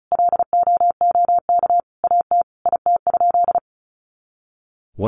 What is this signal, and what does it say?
Morse code, heard as a keyed tone, about 34 wpm, 700 Hz. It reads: LOOK AT IT?
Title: Sentences from Top 100 Words 35wpm